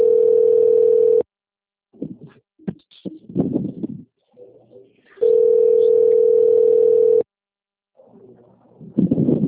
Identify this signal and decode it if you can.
Ringback tone (North America), 440 + 480 Hz.